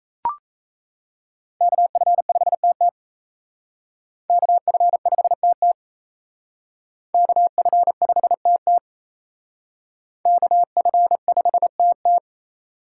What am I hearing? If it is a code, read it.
Morse code, heard as a keyed tone, about 36 wpm, 700 Hz. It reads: KF5TT KF5TT KF5TT KF5TT